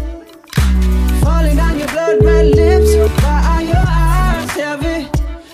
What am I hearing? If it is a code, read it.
Ringback tone (Europe, 425 Hz), with noise.